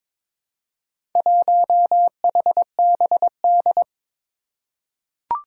Morse code, heard as a keyed tone, about 23 wpm, 700 Hz. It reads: 1HBD
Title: Four Challenging Letter-Number Combinations 22wpm